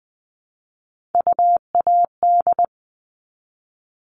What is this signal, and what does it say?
Morse code, heard as a keyed tone, about 20 wpm, 700 Hz. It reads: UAD
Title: All Cut Numbers - 3-Digit 20wpm